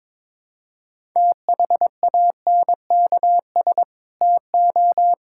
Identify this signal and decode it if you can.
Morse code, heard as a keyed tone, about 22 wpm, 700 Hz. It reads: THANKS TO
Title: Sentences from Top 1000 Words - Review Part 5 - 22wpm